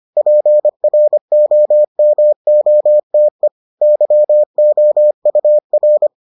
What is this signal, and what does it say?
Morse code, heard as a keyed tone, about 25 wpm, 600 Hz. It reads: PROMOTE YOUR